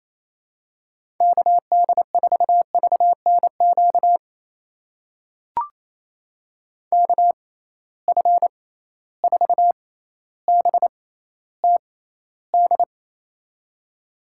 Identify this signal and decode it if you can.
Morse code, heard as a keyed tone, about 28 wpm, spaced as for 14 wpm, 700 Hz. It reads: KD4VNQ KF4BTD
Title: USA Call Signs - 6 Characters - 6x Character Spacing - 28wpm